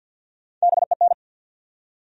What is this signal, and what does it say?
Morse code, heard as a keyed tone, about 52 wpm, 700 Hz. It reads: BEN
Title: USA Common Names - Max 3 Letters - 50wpm